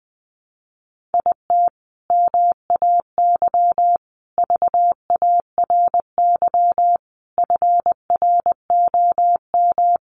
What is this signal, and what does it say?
Morse code, heard as a keyed tone, about 19 wpm, 700 Hz. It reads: IT MAY VARY FROM
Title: Sentences from Top 1200 Words 20wpm